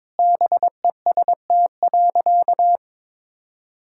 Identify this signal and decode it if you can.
Morse code, heard as a keyed tone, about 22 wpm, 700 Hz. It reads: BEST.
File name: practice / 22wpm / Sentences with 4-Word Phrases - 22wpm